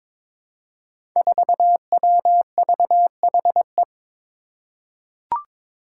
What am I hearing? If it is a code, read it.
Morse code, heard as a keyed tone, about 23 wpm, 700 Hz. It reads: 4WVHE
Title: Five Challenging Letter-Number Combinations 22wpm